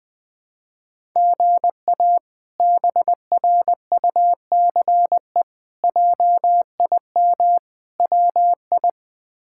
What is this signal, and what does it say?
Morse code, heard as a keyed tone, about 21 wpm, 700 Hz. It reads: GA BRUCE JIM WI
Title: K1USN SST QSO Practice - 20wpm